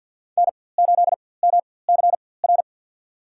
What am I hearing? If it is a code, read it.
Morse code, heard as a keyed tone, about 52 wpm, spaced as for 24 wpm, 700 Hz. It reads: N8MCR
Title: USA Call Signs - 4x Character Spacing - 50wpm